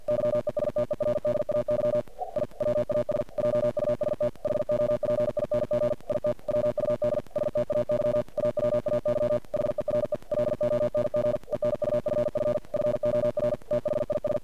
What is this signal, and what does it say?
Morse code, heard as a keyed tone, about 45 wpm, 590 Hz, with noise. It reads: YESTERDAY I WAS JUST HOPING IT WUD STAY AWAY HERE LONG ENUFF FOR THIS
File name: archived / CRSnet